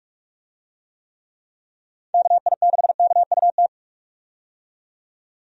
Morse code, heard as a keyed tone, about 45 wpm, 700 Hz. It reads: KI6KUT
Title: USA Call Signs - 6 Characters - 45wpm